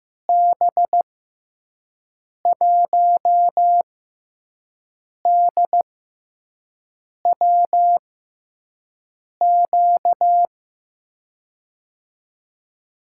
Morse code, heard as a keyed tone, about 15 wpm, spaced as for 5 wpm, 700 Hz. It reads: B1DWQ
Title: USA Call Signs - 6 Characters - 6x Character Spacing - 15wpm